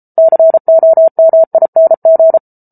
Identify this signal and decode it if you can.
Morse code, heard as a keyed tone, about 34 wpm, 650 Hz. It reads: COMING